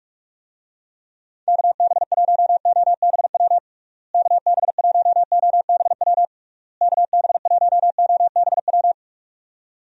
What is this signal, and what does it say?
Morse code, heard as a keyed tone, about 46 wpm, 700 Hz. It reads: KB1OBW KB1OBW KB1OBW
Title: USA Call Signs - 6 Characters - Repeated 3x - 45wpm